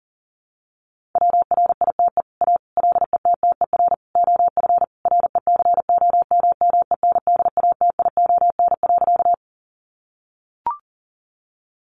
Morse code, heard as a keyed tone, about 37 wpm, 700 Hz. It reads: WRITE A LETTER OF RECOMMENDATION.
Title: Sentences from Top 1400 Words 40wpm